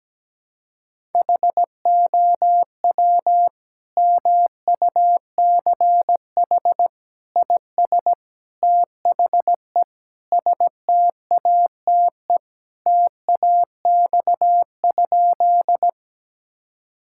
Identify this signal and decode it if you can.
Morse code, heard as a keyed tone, about 17 wpm, 700 Hz. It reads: HOW MUCH IS THE STATE TAX?
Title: Sentences from Top 1000 Words - Review Part 5 - 17wpm